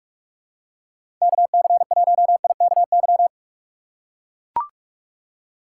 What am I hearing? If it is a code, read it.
Morse code, heard as a keyed tone, about 42 wpm, 700 Hz. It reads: KC1IKY
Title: USA Call Signs - 6 Characters - 3x Character Spacing - 45wpm